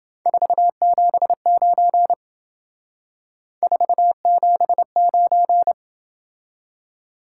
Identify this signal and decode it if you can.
Morse code, heard as a keyed tone, about 29 wpm, 700 Hz. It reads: 479 479